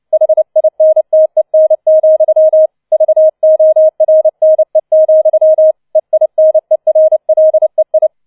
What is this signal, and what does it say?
Morse code, heard as a keyed tone, about 29 wpm, 630 Hz. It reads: HINTEN, VORNE, EINERLEI